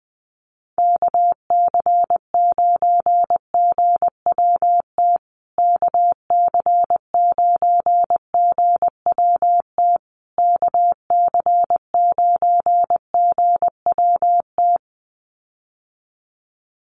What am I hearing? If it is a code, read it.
Morse code, heard as a keyed tone, about 20 wpm, 700 Hz. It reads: KC9GWT KC9GWT KC9GWT
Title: USA Call Signs Repeated 3x 20wpm